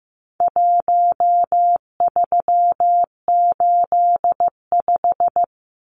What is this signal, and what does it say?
Morse code, heard as a keyed tone, about 15 wpm, 700 Hz. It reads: 1385
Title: Numbers - 4-Digit 15wpm